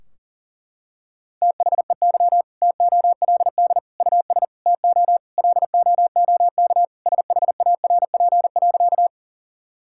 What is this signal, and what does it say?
Morse code, heard as a keyed tone, about 37 wpm, 700 Hz. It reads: THEY TOLD US TO LOOK SHARP.
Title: Sentences from Top 1600 Words 40wpm